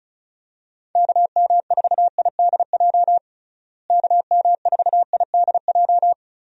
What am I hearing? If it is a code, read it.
Morse code, heard as a keyed tone, about 34 wpm, 700 Hz. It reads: KM4IDJ KM4IDJ
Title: USA Call Signs - 6 Characters - Repeated 3x - 35wpm